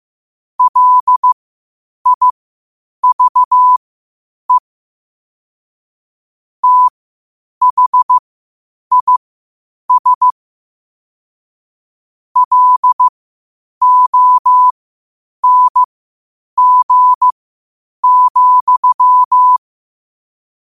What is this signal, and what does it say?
Morse code, heard as a keyed tone, about 15 wpm, spaced as for 8 wpm, 1000 Hz. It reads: LIVE THIS LONG,
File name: qotdfiles / 8wpm